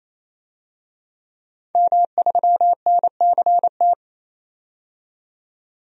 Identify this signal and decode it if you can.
Morse code, heard as a keyed tone, about 28 wpm, 700 Hz. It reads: M3NCT
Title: DXCC-100 Call Signs - 5 Characters - 28wpm